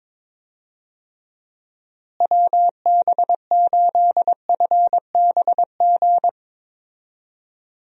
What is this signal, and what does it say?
Morse code, heard as a keyed tone, about 22 wpm, 700 Hz. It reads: WB8FBG
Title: USA Call Signs - 6 Characters - Repeated 2x - 22wpm